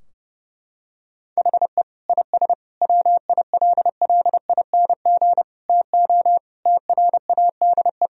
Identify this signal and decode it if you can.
Morse code, heard as a keyed tone, about 32 wpm, 700 Hz. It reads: HE IS WILLING TO TRADE
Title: Sentences from Top 1000 Words - Review Part 4 - 30wpm